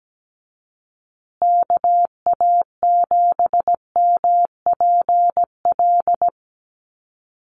Morse code, heard as a keyed tone, about 17 wpm, 700 Hz. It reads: KA7MPL
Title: USA Call Signs Repeated 2x 17wpm